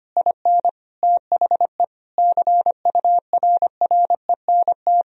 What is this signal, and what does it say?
Morse code, heard as a keyed tone, about 25 wpm, 700 Hz. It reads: IN THE CURRENT
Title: Sentences from Top 1000 Words - Review Part 4 - 25wpm